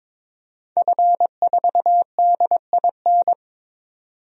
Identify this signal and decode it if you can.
Morse code, heard as a keyed tone, about 22 wpm, 700 Hz. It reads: F4DIN
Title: DXCC-100 Call Signs - 5 Characters - 22wpm